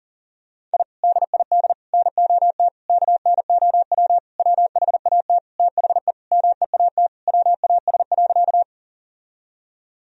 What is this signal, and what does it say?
Morse code, heard as a keyed tone, about 40 wpm, 700 Hz. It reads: I DID NOT KNOW WHAT THE MEAT WAS.
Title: Sentences from Top 1000 Words - Review Part 3 - 40wpm